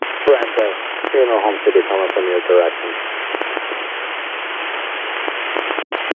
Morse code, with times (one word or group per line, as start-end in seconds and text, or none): none